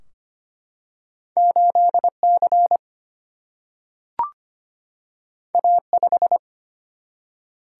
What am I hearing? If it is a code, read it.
Morse code, heard as a keyed tone, about 25 wpm, 700 Hz. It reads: 8C A5
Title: Single-Letter Single-Number 25wpm